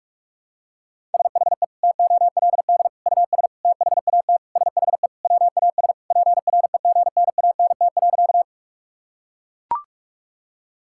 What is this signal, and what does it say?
Morse code, heard as a keyed tone, about 41 wpm, 690 Hz. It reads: SHE TOLD US THAT SHE WAS PREGNANT.
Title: Sentences from Top 1000 Words - Review Part 6 - 45wpm